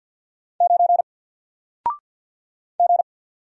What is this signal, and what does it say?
Morse code, heard as a keyed tone, about 50 wpm, 700 Hz. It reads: 9 G
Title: Single Letter-Number 50wpm